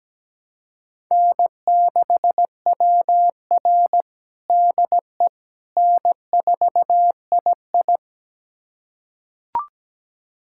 Morse code, heard as a keyed tone, about 17 wpm, 700 Hz. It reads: N6WR DE N4II
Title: USA Callsign DE Callsign - 4 Characters - 17wpm